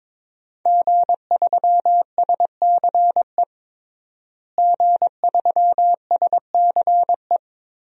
Morse code, heard as a keyed tone, about 21 wpm, 700 Hz. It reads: G3SCE G3SCE
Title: DXCC-100 Call Signs - 5 Characters - Repeated 2x - 22wpm